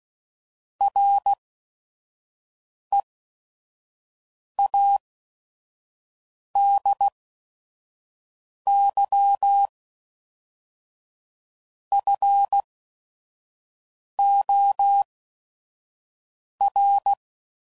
Morse code, heard as a keyed tone, about 16 wpm, spaced as for 5 wpm, 800 Hz. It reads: READY FOR
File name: CW Code Groups @5WPM